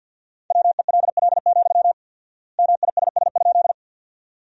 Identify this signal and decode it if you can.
Morse code, heard as a keyed tone, about 52 wpm, 700 Hz. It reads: WELL, MISS?
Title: 4x Word-Spacing - Sentences from Top 500 Words - 50wpm